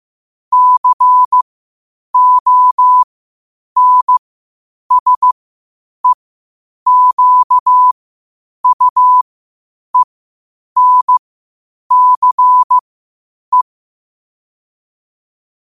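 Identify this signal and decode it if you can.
Morse code, heard as a keyed tone, about 15 wpm, spaced as for 8 wpm, 1000 Hz. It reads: CONSEQUENCE
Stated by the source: Morse practice at 8 WPM from Morse Resource.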